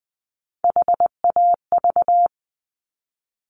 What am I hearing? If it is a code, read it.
Morse code, heard as a keyed tone, about 19 wpm, 700 Hz. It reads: HAV
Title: Top Three Letter Combinations in Words 20wpm